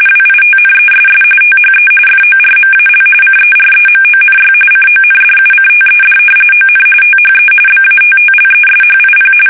RTTY, about 45.45 baud. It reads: WITNESS THE FORMAL CEREMONY CONDUCTED ON THE BATTLESHIP
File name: fsk850 / usnavy1